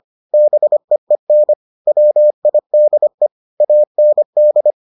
Morse code, heard as a keyed tone, about 25 wpm, 600 Hz. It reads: BEEN WIDE AND